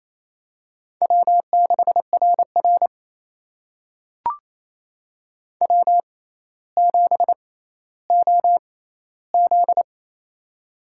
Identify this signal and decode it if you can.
Morse code, heard as a keyed tone, about 27 wpm, spaced as for 14 wpm, 700 Hz. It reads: W6RR W7OZ
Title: USA Call Signs - 4 Characters - 6x Character Spacing - 28wpm